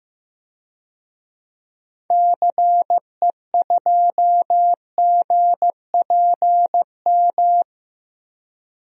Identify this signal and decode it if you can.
Morse code, heard as a keyed tone, about 15 wpm, 700 Hz. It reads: CE2GPM
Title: DXCC-100 Call Signs - 6 Characters - Repeated 3x - 15wpm